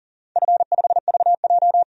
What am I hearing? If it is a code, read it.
Morse code, heard as a keyed tone, about 39 wpm, 700 Hz. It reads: F5VJ
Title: DXCC-100 Call Signs - 4 Characters - 40wpm